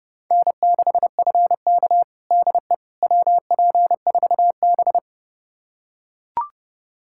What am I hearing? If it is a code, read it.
Morse code, heard as a keyed tone, about 32 wpm, 700 Hz. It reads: N6FK DE WP4B
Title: USA Callsign DE Callsign - 4 Characters - 30wpm